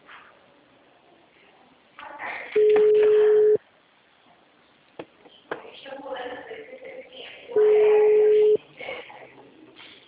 Ringback tone (Europe, 425 Hz).
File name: out- -2003- - - .80123